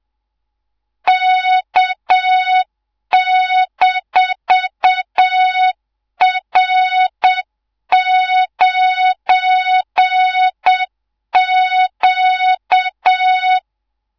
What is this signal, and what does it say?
Morse code, heard as a keyed tone, about 7 wpm, 3700 Hz, with noise. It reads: K-R9Q
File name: Morze-HA2MN / hangp / 2 03 GYAKORLAT